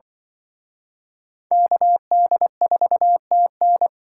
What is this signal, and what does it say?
Morse code, heard as a keyed tone, about 24 wpm, 700 Hz. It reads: KD4TN
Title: Speed Racing - USA Call Signs 20wpm